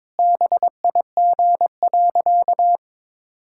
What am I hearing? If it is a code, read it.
Morse code, heard as a keyed tone, about 22 wpm, 700 Hz. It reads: BIG.
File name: practice / 22wpm / Sentences with 2-Word Phrases - 22wpm